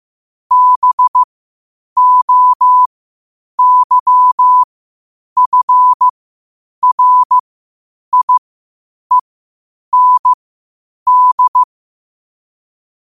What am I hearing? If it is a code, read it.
Morse code, heard as a keyed tone, about 15 wpm, spaced as for 8 wpm, 1000 Hz. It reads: BOYFRIEND
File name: qotdfiles / 8wpm